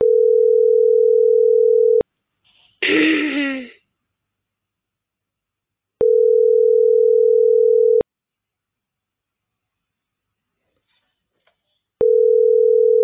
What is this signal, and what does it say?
Ringback tone (North America), 440 + 480 Hz.